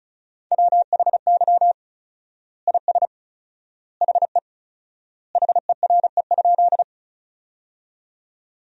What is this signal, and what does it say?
Morse code, heard as a keyed tone, about 34 wpm, 700 Hz. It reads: WHY IS HE HERE?
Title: 4x Word-Spacing - Sentences from Top 200 Words - 35wpm